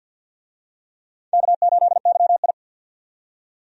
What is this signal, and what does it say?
Morse code, heard as a keyed tone, about 47 wpm, 700 Hz. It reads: K8YI
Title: USA Call Signs - 4 Characters - 50wpm